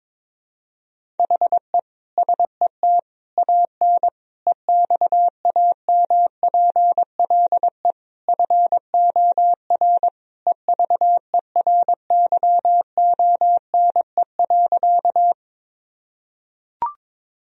Morse code, heard as a keyed tone, about 22 wpm, 700 Hz. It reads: HE SET AN EXAMPLE FOR EVERYONE.
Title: Sentences from Top 1000 Words - Review Part 1 - 22wpm